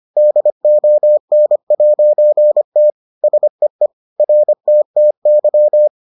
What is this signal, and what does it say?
Morse code, heard as a keyed tone, about 25 wpm, 600 Hz. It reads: DON'T SEE RTTY